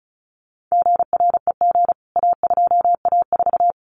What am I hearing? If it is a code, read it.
Morse code, heard as a keyed tone, about 34 wpm, 700 Hz. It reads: GREG A2A4
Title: CWT Exchange 35wpm